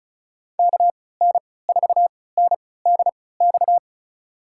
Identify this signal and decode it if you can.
Morse code, heard as a keyed tone, about 36 wpm, spaced as for 20 wpm, 700 Hz. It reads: KN4NDX
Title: USA Call Signs - 6 Characters - 3x Character Spacing - 35wpm